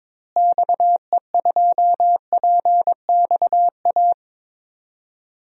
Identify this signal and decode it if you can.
Morse code, heard as a keyed tone, about 22 wpm, 700 Hz. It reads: XE2PXA
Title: DXCC-100 Call Signs - 6 Characters - Repeated 3x - 22wpm